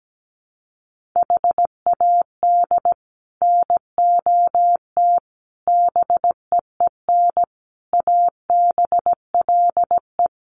Morse code, heard as a keyed tone, about 17 wpm, 700 Hz. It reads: HAD NOT BEEN ABLE
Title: Sets of 5 Words Encore 17wpm